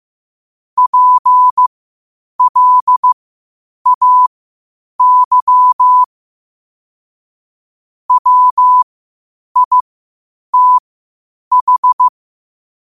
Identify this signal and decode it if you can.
Morse code, heard as a keyed tone, about 15 wpm, spaced as for 8 wpm, 1000 Hz. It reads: PLAY WITH
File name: qotdfiles / 8wpm